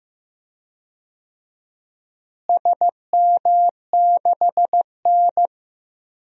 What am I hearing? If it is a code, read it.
Morse code, heard as a keyed tone, about 15 wpm, 700 Hz. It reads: SM6N
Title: DXCC-100 Call Signs - 4 Characters - Repeated 3x - 15wpm